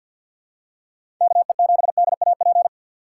Morse code, heard as a keyed tone, about 48 wpm, 700 Hz. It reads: KE7DAP
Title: USA Call Signs - 6 Characters - 6x Character Spacing - 50wpm